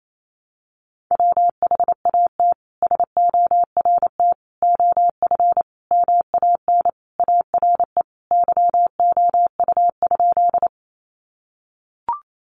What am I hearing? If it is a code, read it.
Morse code, heard as a keyed tone, about 28 wpm, 700 Hz. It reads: WHAT SORT OF MAN ARE YOU?
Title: Sentences from Top 1300 Words 28wpm